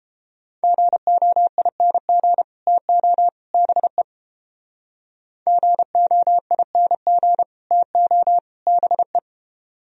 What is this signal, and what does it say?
Morse code, heard as a keyed tone, about 32 wpm, 700 Hz. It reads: GOING TO BE GOING TO BE